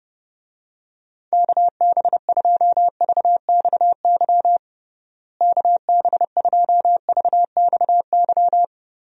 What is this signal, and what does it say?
Morse code, heard as a keyed tone, about 30 wpm, 700 Hz. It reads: KB2VXY KB2VXY
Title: USA Call Signs - 6 Characters - Repeated 3x - 30wpm